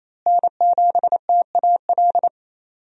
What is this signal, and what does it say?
Morse code, heard as a keyed tone, about 28 wpm, 700 Hz. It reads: N7TAL
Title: USA Call Signs - 5 Characters - Repeated 2x - 28wpm